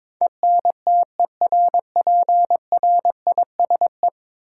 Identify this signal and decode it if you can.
Morse code, heard as a keyed tone, about 22 wpm, 700 Hz. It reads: ENTERPRISE
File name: practice / 22wpm / Ten Letter Words - 22wpm